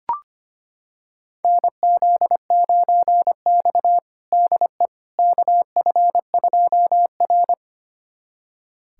Morse code, heard as a keyed tone, about 25 wpm, 700 Hz. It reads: NZ9X DE KF2R
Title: USA Callsign DE Callsign - 4 Characters - 25wpm